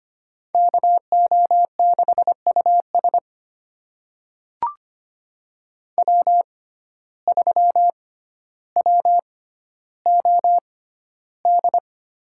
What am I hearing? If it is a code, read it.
Morse code, heard as a keyed tone, about 25 wpm, spaced as for 13 wpm, 700 Hz. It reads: KO6US W3WOD
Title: USA Call Signs - 5 Characters - 6x Character Spacing - 25wpm